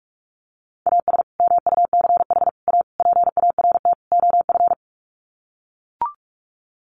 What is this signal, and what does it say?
Morse code, heard as a keyed tone, about 44 wpm, 700 Hz. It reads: AS MUCH A PART OF
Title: Sets of 5 Words Encore2 45wpm